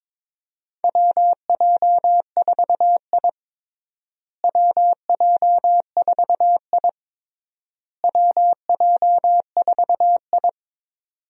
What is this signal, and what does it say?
Morse code, heard as a keyed tone, about 23 wpm, 700 Hz. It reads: WJ4I WJ4I WJ4I
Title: USA Call Signs - 4 Characters - Repeated 3x - 22wpm